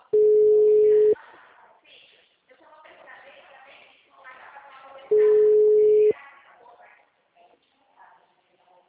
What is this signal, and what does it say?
Ringback tone (Europe, 425 Hz), with speech.